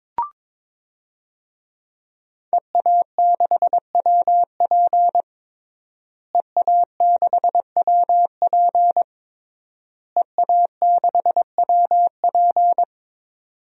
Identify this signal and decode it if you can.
Morse code, heard as a keyed tone, about 22 wpm, 700 Hz. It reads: EA6WP EA6WP EA6WP
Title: DXCC-100 Call Signs - 5 Characters - Repeated 3x - 22wpm